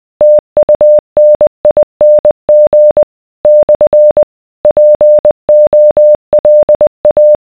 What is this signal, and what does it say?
Morse code, heard as a keyed tone, about 20 wpm, 600 Hz. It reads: TUNING / POLA